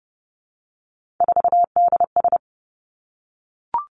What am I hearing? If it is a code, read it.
Morse code, heard as a keyed tone, about 30 wpm, 700 Hz. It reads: 4DS
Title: Three Challenging Letter-Number Combinations 30wpm